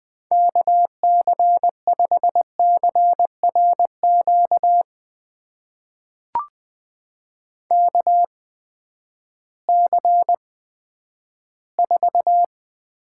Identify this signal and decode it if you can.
Morse code, heard as a keyed tone, about 20 wpm, spaced as for 11 wpm, 700 Hz. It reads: KC5CRQ KC4
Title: USA Call Signs - 6 Characters - 8x Character Spacing - 20wpm